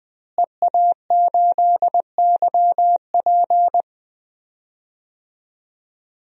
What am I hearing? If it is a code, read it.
Morse code, heard as a keyed tone, about 20 wpm, 700 Hz. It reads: EA8YP
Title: DXCC-100 Call Signs - 5 Characters - Repeated 3x - 20wpm